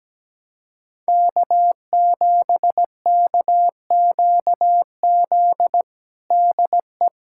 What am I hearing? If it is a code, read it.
Morse code, heard as a keyed tone, about 17 wpm, 700 Hz. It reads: K7KQZ DE
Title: USA Callsign DE Callsign 17wpm